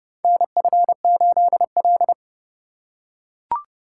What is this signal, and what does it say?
Morse code, heard as a keyed tone, about 30 wpm, 700 Hz. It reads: NF8L
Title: USA Call Signs - 3x Character Spacing - 30wpm